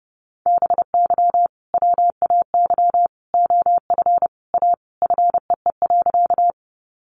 Morse code, heard as a keyed tone, about 30 wpm, 700 Hz. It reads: BY WAY OF A FEE.
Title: Sentences from Top 1500 Words 30wpm